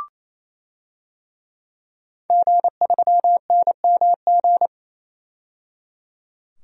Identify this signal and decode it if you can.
Morse code, heard as a keyed tone, about 28 wpm, 700 Hz. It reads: G3NMG
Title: DXCC-100 Call Signs - 5 Characters - 28wpm